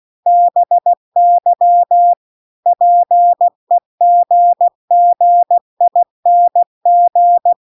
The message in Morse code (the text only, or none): BY PEGGING